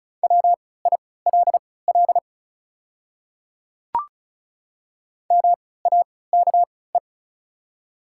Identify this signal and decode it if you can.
Morse code, heard as a keyed tone, about 36 wpm, spaced as for 20 wpm, 690 Hz. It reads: WILL MAKE
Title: 3x Character Spacing - Top 100 Words - 35wpm